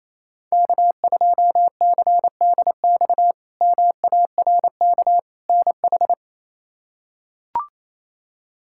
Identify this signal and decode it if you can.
Morse code, heard as a keyed tone, about 27 wpm, 700 Hz. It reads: K2CDX MARK NH
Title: K1USN SST QSO Practice - 28wpm